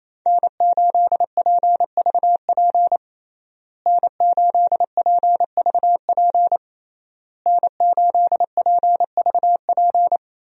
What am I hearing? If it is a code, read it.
Morse code, heard as a keyed tone, about 28 wpm, 700 Hz. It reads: N8PVP N8PVP N8PVP
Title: USA Call Signs - 5 Characters - Repeated 3x - 28wpm